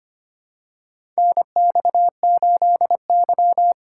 Morse code, heard as a keyed tone, about 25 wpm, 700 Hz. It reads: NX8Y
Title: USA Call Signs - 6x Character Spacing - 25wpm